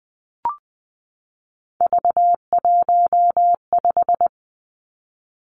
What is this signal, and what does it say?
Morse code, heard as a keyed tone, about 20 wpm, 700 Hz. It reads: V15
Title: Three Challenging Letter-Number Combinations 20wpm